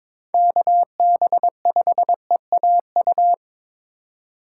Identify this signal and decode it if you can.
Morse code, heard as a keyed tone, about 22 wpm, 700 Hz. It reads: KB5EAU
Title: USA Call Signs - 6 Characters - 4x Character Spacing - 22wpm